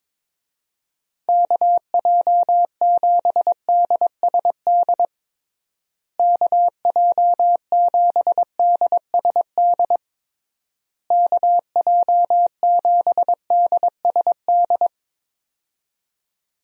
Morse code, heard as a keyed tone, about 22 wpm, 700 Hz. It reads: KJ7DSD KJ7DSD KJ7DSD
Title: USA Call Signs - 6 Characters - Repeated 3x - 22wpm